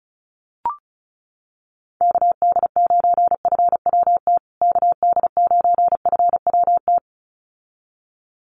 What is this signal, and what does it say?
Morse code, heard as a keyed tone, about 34 wpm, 700 Hz. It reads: KD9FWT KD9FWT
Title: USA Call Signs Repeated 2x 35wpm